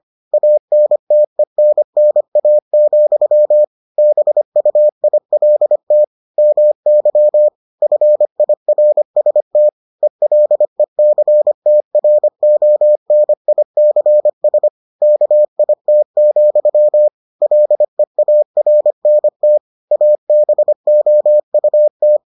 Morse code, heard as a keyed tone, about 25 wpm, 600 Hz. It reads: ANTENNA, BUILT MY FIRST ELECTRONICS KIT, LEARNT ABOUT